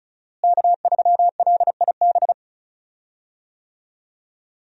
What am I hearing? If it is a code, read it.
Morse code, heard as a keyed tone, about 34 wpm, 700 Hz. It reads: K3LIB
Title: USA Call Signs - 5 Characters - Repeated 3x - 35wpm